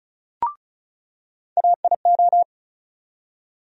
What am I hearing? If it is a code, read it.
Morse code, heard as a keyed tone, about 36 wpm, 700 Hz. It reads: AIO
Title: USA Call Sign Postfixes - 3 Letters 35wpm